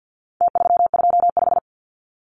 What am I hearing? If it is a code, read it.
Morse code, heard as a keyed tone, about 50 wpm, 700 Hz. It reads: T325
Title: Numbers - Contest Style - 4-Digit 50wpm